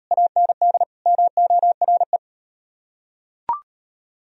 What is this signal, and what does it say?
Morse code, heard as a keyed tone, about 36 wpm, 700 Hz. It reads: AND MORE